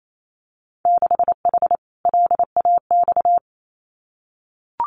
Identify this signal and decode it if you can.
Morse code, heard as a keyed tone, about 28 wpm, 700 Hz. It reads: 6H LAX
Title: Winter Field Day Exchanges Encore 28wpm